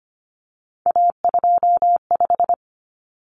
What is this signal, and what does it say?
Morse code, heard as a keyed tone, about 25 wpm, 700 Hz. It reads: A25
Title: Common Cut Numbers - 3-Digit 25wpm